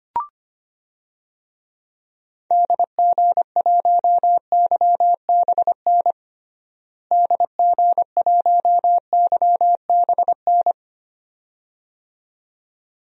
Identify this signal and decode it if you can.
Morse code, heard as a keyed tone, about 25 wpm, 700 Hz. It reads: DG1YBN DG1YBN
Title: DXCC-100 Call Signs - 6 Characters - Repeated 2x - 25wpm